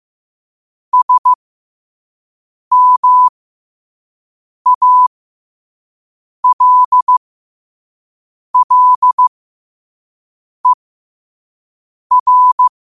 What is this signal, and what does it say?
Morse code, heard as a keyed tone, about 15 wpm, spaced as for 5 wpm, 1000 Hz. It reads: SMALLER
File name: qotdfiles / 5wpm